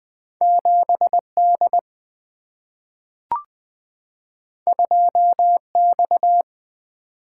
Morse code, heard as a keyed tone, about 20 wpm, 700 Hz. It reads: 7D 2X
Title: Single-Letter Single-Number 20wpm